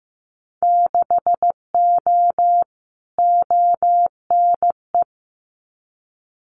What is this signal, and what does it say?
Morse code, heard as a keyed tone, about 15 wpm, 700 Hz. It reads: 6O ONE
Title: Winter Field Day Exchanges Encore 15wpm